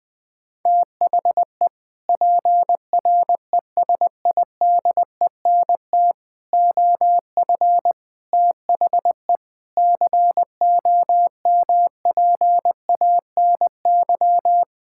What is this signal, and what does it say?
Morse code, heard as a keyed tone, about 20 wpm, 700 Hz. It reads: THE PRESIDENT OF THE COMPANY
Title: Sentences with 3-Word Phrases - Spoken Beginning - 20wpm